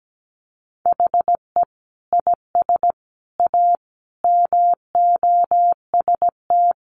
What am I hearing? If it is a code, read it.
Morse code, heard as a keyed tone, about 17 wpm, 700 Hz. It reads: HE IS A MOST
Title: Sentences from Top 1000 Words 17wpm